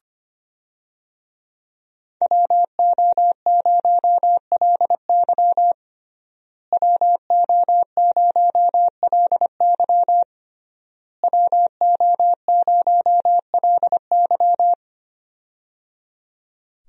Morse code, heard as a keyed tone, about 25 wpm, 700 Hz. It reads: WO0LY WO0LY WO0LY
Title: USA Call Signs - 5 Characters - Repeated 3x - 25wpm